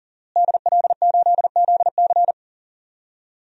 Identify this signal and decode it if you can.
Morse code, heard as a keyed tone, about 38 wpm, 700 Hz. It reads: DL8ZC
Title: DXCC-100 Call Signs - 5 Characters - Repeated 3x - 40wpm